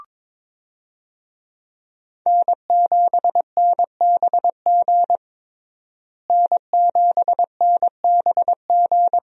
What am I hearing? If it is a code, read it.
Morse code, heard as a keyed tone, about 22 wpm, 700 Hz. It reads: N7NBG N7NBG
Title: USA Call Signs - 5 Characters - Repeated 2x - 22wpm